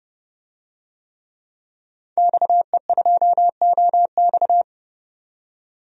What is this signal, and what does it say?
Morse code, heard as a keyed tone, about 32 wpm, 700 Hz. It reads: XE2OX
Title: DXCC-100 Call Signs - 5 Characters - 30wpm